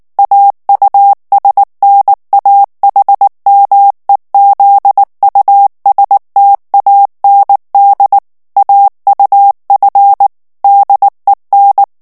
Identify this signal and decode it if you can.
Morse code, heard as a keyed tone, about 19 wpm, 800 Hz. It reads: AUSNAHMEZUSTAND AUF DEN
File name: morse mp3 / ALLEMAND texte clair 2 vitesse 18 mots minute